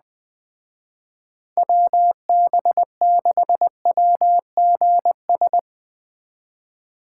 Morse code, heard as a keyed tone, about 20 wpm, 700 Hz. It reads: WB6WGS